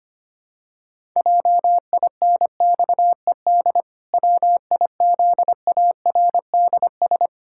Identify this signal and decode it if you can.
Morse code, heard as a keyed tone, about 25 wpm, 700 Hz. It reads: JINXED WIZARDS
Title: Pangram Challenge v2 - 25wpm